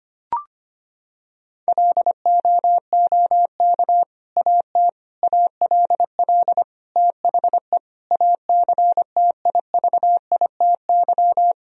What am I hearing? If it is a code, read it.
Morse code, heard as a keyed tone, about 25 wpm, 700 Hz. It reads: LOOK AT ALL THE ACTIVITY
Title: Sentences from Top 1000 Words - Review Part 4 - 25wpm